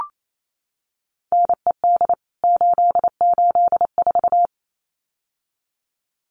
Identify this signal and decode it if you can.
Morse code, heard as a keyed tone, about 28 wpm, 700 Hz. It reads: NED 884
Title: CWT Exchange 28wpm